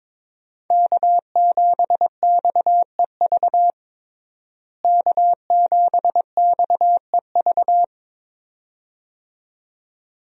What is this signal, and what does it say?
Morse code, heard as a keyed tone, about 22 wpm, 700 Hz. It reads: K7XEV K7XEV
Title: USA Call Signs - 5 Characters - Repeated 2x - 22wpm